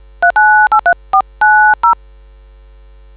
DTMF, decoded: 3983490